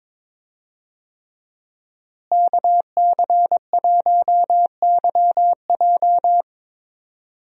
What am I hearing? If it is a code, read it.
Morse code, heard as a keyed tone, about 22 wpm, 700 Hz. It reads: KC1YJ
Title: USA Call Signs - 5 Characters - Repeated 3x - 22wpm